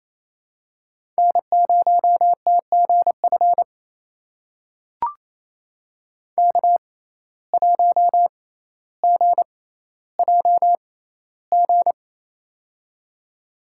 Morse code, heard as a keyed tone, about 28 wpm, spaced as for 14 wpm, 700 Hz. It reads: N0TGF K1GJG
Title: USA Call Signs - 5 Characters - 6x Character Spacing - 28wpm